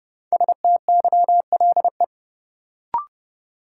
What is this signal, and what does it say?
Morse code, heard as a keyed tone, about 30 wpm, 700 Hz. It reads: STYLE
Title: Sentences from Top 800 Words - New Word Drill - 30wpm